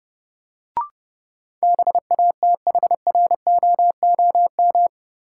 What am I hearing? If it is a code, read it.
Morse code, heard as a keyed tone, about 29 wpm, 700 Hz. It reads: BATHROOM
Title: Sets of 5 Words Encore - New Word Drill - 30wpm